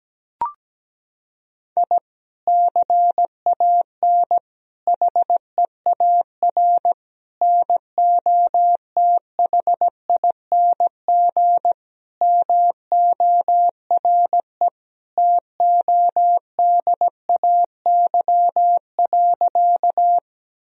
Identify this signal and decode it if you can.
Morse code, heard as a keyed tone, about 17 wpm, 700 Hz. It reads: I CAN HEAR NOTHING MORE TODAY.
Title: Sentences from Top 1000 Words - Review Part 4 - 17wpm